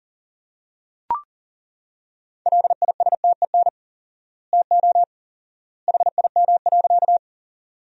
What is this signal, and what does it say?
Morse code, heard as a keyed tone, about 40 wpm, 700 Hz. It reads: LISTEN TO HIM.
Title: 4x Word-Spacing - Sentences from Top 400 Words - 40wpm